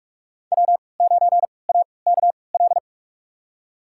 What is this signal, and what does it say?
Morse code, heard as a keyed tone, about 46 wpm, spaced as for 25 wpm, 700 Hz. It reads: W9AKL
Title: USA Call Signs - 5 Characters - 3x Character Spacing - 45wpm